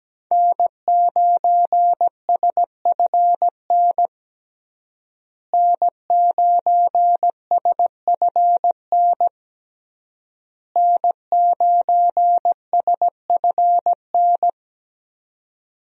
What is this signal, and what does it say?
Morse code, heard as a keyed tone, about 17 wpm, 700 Hz. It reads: N9SFN N9SFN N9SFN
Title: USA Call Signs - 5 Characters - Repeated 3x - 17wpm